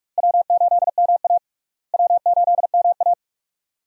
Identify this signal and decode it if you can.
Morse code, heard as a keyed tone, about 46 wpm, 700 Hz. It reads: W8MA W8MA
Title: USA Call Signs - 4 Characters - Repeated 2x - 45wpm